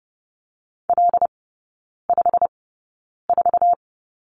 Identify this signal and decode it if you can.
Morse code, heard as a keyed tone, about 30 wpm, spaced as for 9 wpm, 700 Hz. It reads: L54
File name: Lesson 15e - Number 5 - Three Characters 30wpm